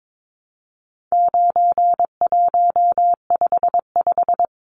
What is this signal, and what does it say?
Morse code, heard as a keyed tone, about 22 wpm, 700 Hz. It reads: 9155
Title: Numbers - 4-Digit 22wpm